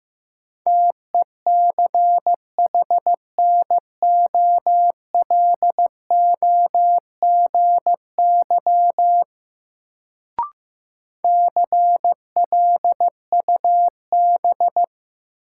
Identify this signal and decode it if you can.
Morse code, heard as a keyed tone, about 15 wpm, 700 Hz. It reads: TECHNOLOGY CLUB
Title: Sentences from Top 1200 Words - New Word Drill Alt - 15wpm